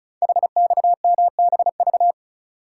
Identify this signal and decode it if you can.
Morse code, heard as a keyed tone, about 36 wpm, 690 Hz. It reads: HXMBV
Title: Code Groups - 5 Letters - 35wpm